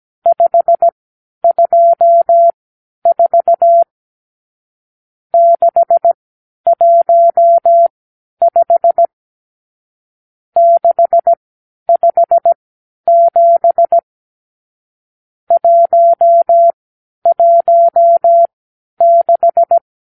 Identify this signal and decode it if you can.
Morse code, heard as a keyed tone, about 17 wpm, spaced as for 10 wpm, 680 Hz. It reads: 524 615 657 116
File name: k7qo / Lesson 52 (8)